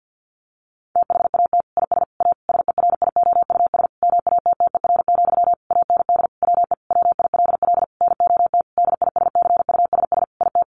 Morse code, heard as a keyed tone, about 50 wpm, 700 Hz. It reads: THAT IS A SERIOUS MATTER, AND WE WILL NOT DISCUSS IT
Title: Sentences from Top 1100 Words 50wpm